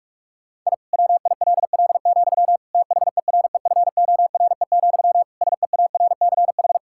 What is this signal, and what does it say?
Morse code, heard as a keyed tone, about 42 wpm, 700 Hz. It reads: I WILL, THEREFORE, SEARCH
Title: Sentences from Top 1000 Words - Review Part 5 - 45wpm